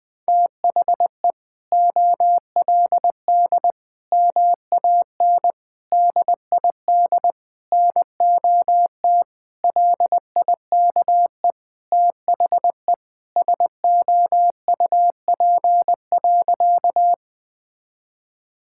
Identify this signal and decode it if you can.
Morse code, heard as a keyed tone, about 20 wpm, 700 Hz. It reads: THE OLD MAN DID NOT LIKE THE SOUP.
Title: Sentences from Top 1000 Words - Review Part 6 - 20wpm